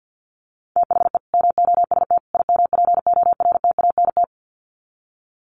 Morse code, heard as a keyed tone, about 50 wpm, 700 Hz. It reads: THE MOST IMPORTANT
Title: Sets of 3 Words 50wpm